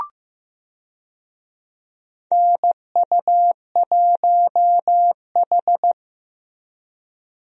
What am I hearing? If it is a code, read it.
Morse code, heard as a keyed tone, about 15 wpm, 700 Hz. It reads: NU1H
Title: USA Call Signs - 4 Characters - Repeated 3x - 15wpm